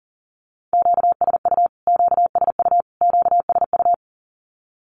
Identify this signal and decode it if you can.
Morse code, heard as a keyed tone, about 40 wpm, 700 Hz. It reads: QSU QSU QSU
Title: Q-Codes Repeated 3x 40wpm